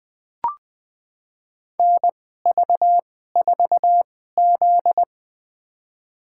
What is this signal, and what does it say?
Morse code, heard as a keyed tone, about 20 wpm, spaced as for 14 wpm, 700 Hz. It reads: NV4Z
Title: USA Call Signs - 4 Characters - 2x Character Spacing - 20wpm